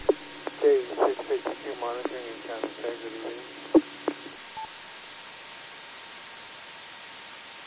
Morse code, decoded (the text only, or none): none